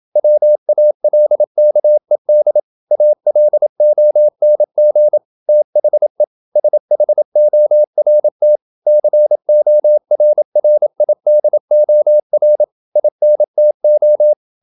Morse code, heard as a keyed tone, about 28 wpm, 600 Hz. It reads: WALKED ALONG THE SHORT CORRIDOR INTO
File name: Never 0027wpm